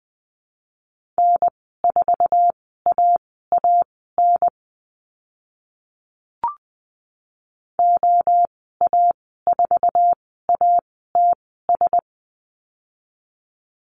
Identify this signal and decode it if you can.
Morse code, heard as a keyed tone, about 20 wpm, spaced as for 14 wpm, 700 Hz. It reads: N4AAN OA4ATS
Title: 069 Lesson 12g - Letter d - Call Signs 20wpm